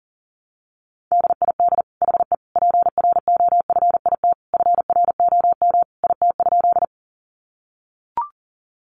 Morse code, heard as a keyed tone, about 40 wpm, 700 Hz. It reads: DID HE PROFIT FROM IT?
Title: Sentences from Top 800 Words 40wpm